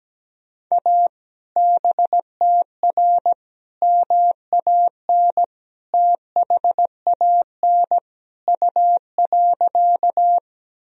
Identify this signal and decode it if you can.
Morse code, heard as a keyed tone, about 17 wpm, 700 Hz. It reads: A BTR MAN THAN U.
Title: Sentences from Top 400 Words and Abbreviations 17wpm